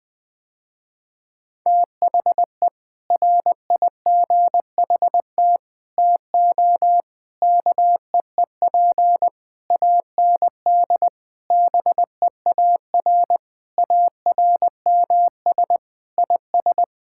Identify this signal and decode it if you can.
Morse code, heard as a keyed tone, about 20 wpm, 700 Hz. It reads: THE RIGHT TO KEEP AND BEAR ARMS IS
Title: Sentences with 7-Word Phrases - Spoken Beginning - 20wpm